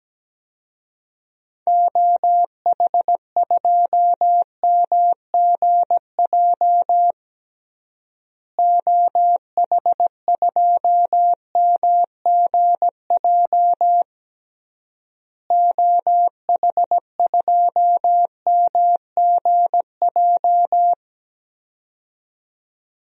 Morse code, heard as a keyed tone, about 17 wpm, 700 Hz. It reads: OH2MGJ OH2MGJ OH2MGJ
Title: DXCC-100 Call Signs - 6 Characters - Repeated 3x - 17wpm